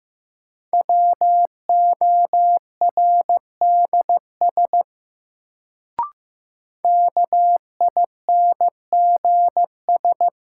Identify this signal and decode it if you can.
Morse code, heard as a keyed tone, about 16 wpm, 700 Hz. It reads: WORDS KINGS
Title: Sentences from Top 1000 Words - Review Part 6 - New Word Drill - 15wpm